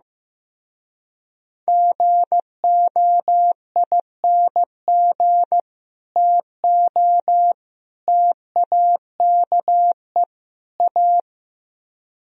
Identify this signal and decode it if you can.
Morse code, heard as a keyed tone, about 15 wpm, 700 Hz. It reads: GOING TO TAKE A
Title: Speed-Racing - Sets of 4 Words 15wpm